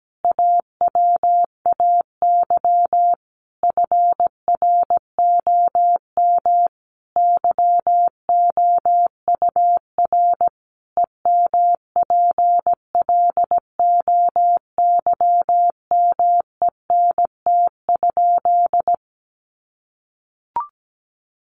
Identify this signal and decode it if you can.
Morse code, heard as a keyed tone, about 17 wpm, 700 Hz. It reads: AWAY FROM YOUR EMPLOYMENT?
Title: Sentences from Top 1500 Words 17wpm